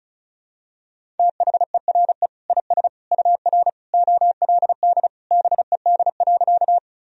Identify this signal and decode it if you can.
Morse code, heard as a keyed tone, about 36 wpm, 700 Hz. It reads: THERE IS UR OLD BED.
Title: Sentences from Top 500 Words and Abbreviations 35wpm